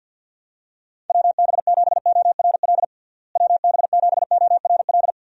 Morse code, heard as a keyed tone, about 45 wpm, 700 Hz. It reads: WB7ORL WB7ORL
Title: USA Call Signs - 6 Characters - Repeated 3x - 50wpm